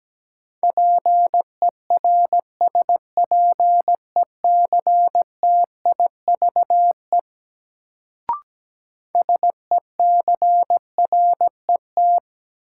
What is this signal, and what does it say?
Morse code, heard as a keyed tone, about 17 wpm, 700 Hz. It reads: PERSPECTIVE SECRET
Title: Sentences from Top 1200 Words - New Word Drill - 17wpm